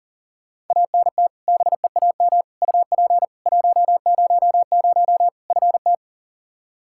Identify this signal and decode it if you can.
Morse code, heard as a keyed tone, about 37 wpm, 700 Hz. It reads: ANT BEAM UP 100 FT
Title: K1USN SST QSO Practice 40wpm